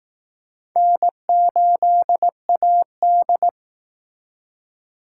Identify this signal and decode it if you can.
Morse code, heard as a keyed tone, about 18 wpm, 700 Hz. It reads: N8AD
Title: Speed Racing - USA 1x2 Call Signs 15wpm